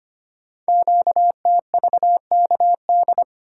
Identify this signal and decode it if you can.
Morse code, heard as a keyed tone, about 25 wpm, 700 Hz. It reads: QTVKD
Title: Code Groups - 5 Letter:Numbers - 25wpm